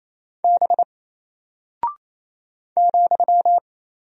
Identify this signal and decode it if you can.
Morse code, heard as a keyed tone, about 27 wpm, 700 Hz. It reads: B ,
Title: Single Character (Letter:Number:Punctuation:Prosign) 28wpm